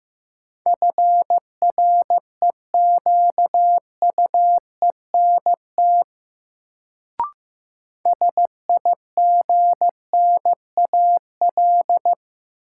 Morse code, heard as a keyed tone, about 15 wpm, 700 Hz. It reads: FREQUENT SIGNAL
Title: Sentences from Top 1400 Words - New Word Drill - 15wpm